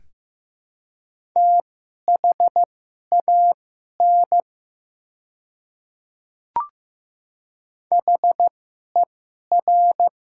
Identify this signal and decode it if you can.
Morse code, heard as a keyed tone, about 15 wpm, spaced as for 11 wpm, 700 Hz. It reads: THAN HER
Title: 2x Character Spacing - Top 200 Words - 15wpm